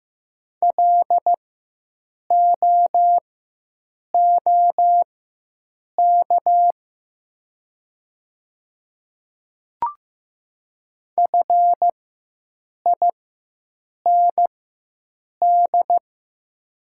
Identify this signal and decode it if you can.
Morse code, heard as a keyed tone, about 15 wpm, spaced as for 7 wpm, 700 Hz. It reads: LOOK FIND
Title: 4x Character Spacing - Top 100 Words - 15wpm